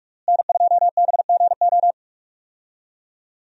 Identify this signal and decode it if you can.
Morse code, heard as a keyed tone, about 46 wpm, 700 Hz. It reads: N2BGO
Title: USA Call Signs - 5 Characters - Repeated 3x - 45wpm